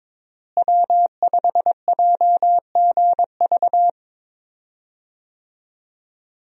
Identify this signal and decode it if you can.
Morse code, heard as a keyed tone, about 22 wpm, 700 Hz. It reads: W5JGV
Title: USA Call Signs - 5 Characters - Repeated 2x - 22wpm